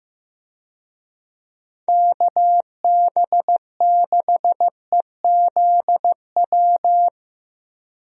Morse code, heard as a keyed tone, about 15 wpm, 700 Hz. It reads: KB6EZW
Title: USA Call Signs - 6 Characters - Repeated 2x - 15wpm